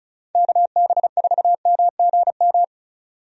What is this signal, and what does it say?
Morse code, heard as a keyed tone, about 36 wpm, 700 Hz. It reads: KB4MGM
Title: USA Call Signs - 6 Characters - 35wpm